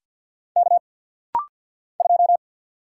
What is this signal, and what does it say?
Morse code, heard as a keyed tone, about 50 wpm, 690 Hz. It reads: K 2
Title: Single Letter-Number - Rapid-Fire 50wpm